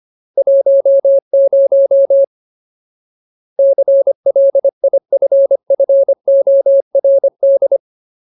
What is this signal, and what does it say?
Morse code, heard as a keyed tone, about 25 wpm, 550 Hz. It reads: 10 CLIFFORD
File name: Ultimate Word List / 25WPM Ultimate